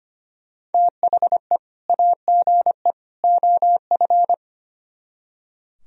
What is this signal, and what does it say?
Morse code, heard as a keyed tone, about 31 wpm, 700 Hz. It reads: THE AGE OF